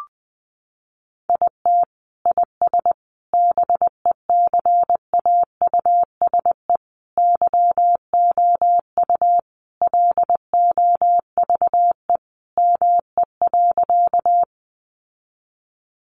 Morse code, heard as a keyed tone, about 19 wpm, 700 Hz. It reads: IT IS BECAUSE YOU LOVE ME.
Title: Sentences from Top 600 Words 20wpm